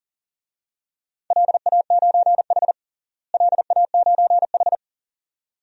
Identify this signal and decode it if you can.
Morse code, heard as a keyed tone, about 40 wpm, 700 Hz. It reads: LA9H LA9H
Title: DXCC-100 Call Signs - 4 Characters - Repeated 2x - 40wpm